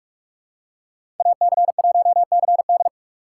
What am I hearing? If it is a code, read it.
Morse code, heard as a keyed tone, about 44 wpm, 700 Hz. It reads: AC1CD
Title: USA Call Signs - 5 Characters - 3x Character Spacing - 45wpm